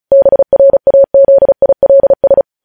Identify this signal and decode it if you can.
Morse code, heard as a keyed tone, about 36 wpm, 550 Hz. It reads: BRAZILS
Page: News Headlines in Morse Code at 35 WPM